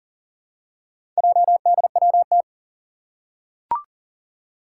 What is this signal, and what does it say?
Morse code, heard as a keyed tone, about 40 wpm, 700 Hz. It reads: JDWT
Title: Four Challenging Letter-Number Combinations 40wpm